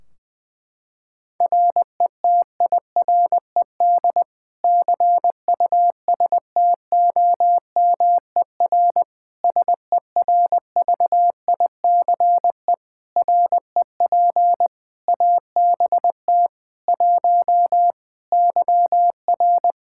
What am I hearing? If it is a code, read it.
Morse code, heard as a keyed tone, about 21 wpm, 700 Hz. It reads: RETIRED CUSTOMER SERVICE REP ABT 1 YR
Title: Practice QSO - Exchange 3 - More Info - 20wpm